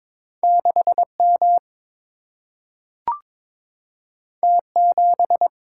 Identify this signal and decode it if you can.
Morse code, heard as a keyed tone, about 22 wpm, 700 Hz. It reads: 6M T7
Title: Single-Letter Single-Number 22wpm